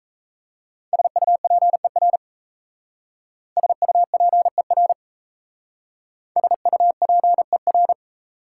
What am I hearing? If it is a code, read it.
Morse code, heard as a keyed tone, about 34 wpm, 690 Hz. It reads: SUPER SUPER SUPER
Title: Speed-Racing Prefixes and Suffixes 28wpm